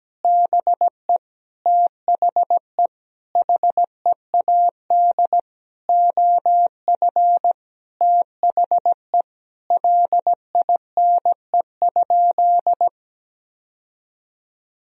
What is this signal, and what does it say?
Morse code, heard as a keyed tone, about 17 wpm, 700 Hz. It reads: BE THE HEAD OF THE LINE?
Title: Sentences from Top 1000 Words - Review Part 1 - 17wpm